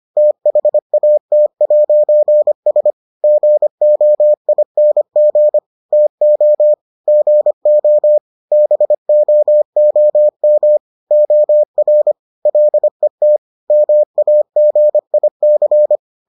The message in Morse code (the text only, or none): THAT'S GOING TO GO BOOM OR LET MAGIC